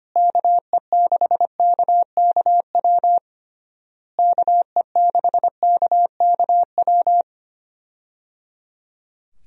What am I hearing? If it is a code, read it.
Morse code, heard as a keyed tone, about 25 wpm, 700 Hz. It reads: KE6KKW KE6KKW
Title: USA Call Signs - 6 Characters - Repeated 3x - 25wpm